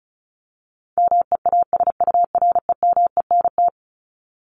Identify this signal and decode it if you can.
Morse code, heard as a keyed tone, about 34 wpm, 700 Hz. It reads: MEASUREMENT
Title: Top 1751-2000 Words 35wpm